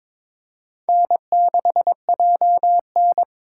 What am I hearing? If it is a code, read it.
Morse code, heard as a keyed tone, about 26 wpm, 700 Hz. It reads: N6JN
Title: Speed Racing - USA Call Signs 22wpm